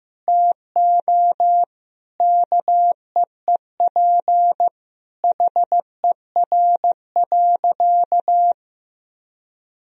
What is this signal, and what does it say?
Morse code, heard as a keyed tone, about 15 wpm, 700 Hz. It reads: TO KEEP HER.
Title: Sentences from Top 1000 Words - Review Part 2 - 15wpm